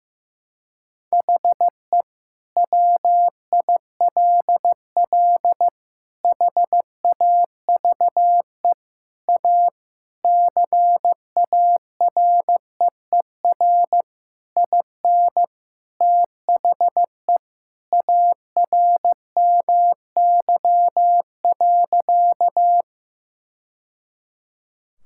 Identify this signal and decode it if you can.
Morse code, heard as a keyed tone, about 15 wpm, 700 Hz. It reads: HE WILL HAVE A CAREER IN THE ARMY.
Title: Sentences from Top 1000 Words - Review Part 6 - 15wpm